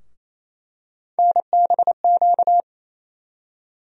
Morse code, heard as a keyed tone, about 28 wpm, 700 Hz. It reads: NBQ
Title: USA Call Sign Postfixes - 3 Letters 28wpm